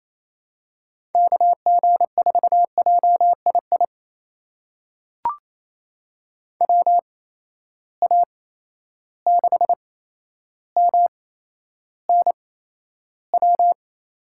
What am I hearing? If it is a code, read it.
Morse code, heard as a keyed tone, about 28 wpm, spaced as for 12 wpm, 700 Hz. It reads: KG4JII WA6MNW
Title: USA Call Signs - 6 Characters - 8x Character Spacing - 28wpm